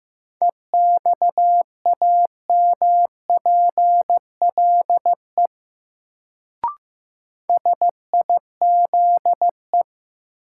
Morse code, heard as a keyed tone, about 15 wpm, 700 Hz. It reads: EXAMPLE SIZE
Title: Sentences from Top 400 Words - New Word Drill - 15wpm